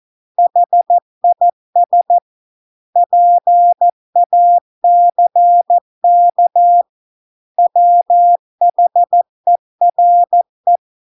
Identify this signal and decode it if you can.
Morse code, heard as a keyed tone, about 14 wpm, 700 Hz. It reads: HIS PACK WHERE